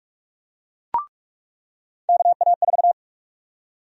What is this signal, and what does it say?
Morse code, heard as a keyed tone, about 42 wpm, 700 Hz. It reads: KA4
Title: USA Call Sign Prefixes - 45wpm